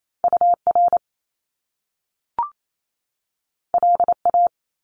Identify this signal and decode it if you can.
Morse code, heard as a keyed tone, about 28 wpm, 700 Hz. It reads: UR LA
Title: Top Two Letter Combinations in Words 28wpm